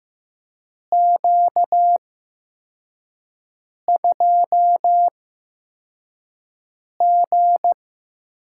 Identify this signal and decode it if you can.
Morse code, heard as a keyed tone, about 15 wpm, spaced as for 4 wpm, 700 Hz. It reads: Q2G
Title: USA Call Signs - 8x Character Spacing - 15wpm